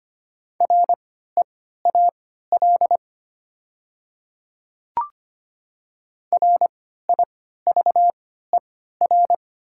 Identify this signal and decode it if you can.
Morse code, heard as a keyed tone, about 25 wpm, spaced as for 14 wpm, 700 Hz. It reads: REAL RIVER
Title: 3x Character Spacing - Top 500 Words - 25wpm